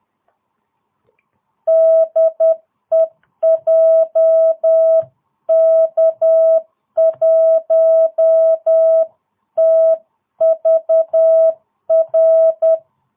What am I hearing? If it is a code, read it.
Morse code, heard as a keyed tone, about 10 wpm, 660 Hz. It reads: DE JK1TVR